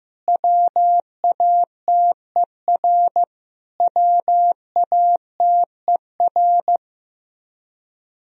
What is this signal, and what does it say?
Morse code, heard as a keyed tone, about 15 wpm, 700 Hz. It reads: WATER WATER
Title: Top 100 Words Repeated 15wpm